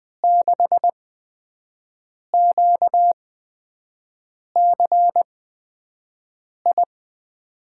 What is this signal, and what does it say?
Morse code, heard as a keyed tone, about 20 wpm, spaced as for 5 wpm, 700 Hz. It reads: 6QCI
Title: USA Call Signs - 5 Characters - 8x Character Spacing - 20wpm